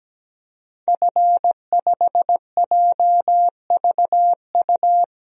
Morse code, heard as a keyed tone, about 17 wpm, 700 Hz. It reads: F5JVU
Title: DXCC-100 Call Signs - 5 Characters - 17wpm